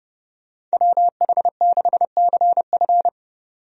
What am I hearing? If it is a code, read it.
Morse code, heard as a keyed tone, about 31 wpm, 700 Hz. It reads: WH6CF
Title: USA Call Signs - 5 Characters - 30wpm